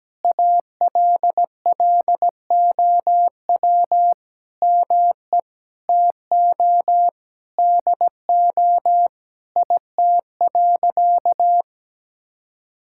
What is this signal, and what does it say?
Morse code, heard as a keyed tone, about 17 wpm, 700 Hz. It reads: ALLOW ME TO DO IT.
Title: Sentences from Top 1000 Words - Review Part 5 - 17wpm